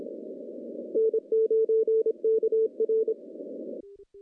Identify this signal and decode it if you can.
Morse code, heard as a keyed tone, about 25 wpm, 440 Hz. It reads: N9KR
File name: contests / CW-SS-k5tr- -0200